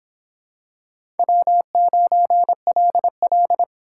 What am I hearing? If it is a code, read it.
Morse code, heard as a keyed tone, about 26 wpm, 700 Hz. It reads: W9LL